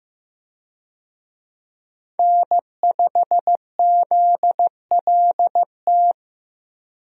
Morse code, heard as a keyed tone, about 15 wpm, 700 Hz. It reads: N5ZLT
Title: USA Call Signs - 5 Characters - Repeated 2x - 15wpm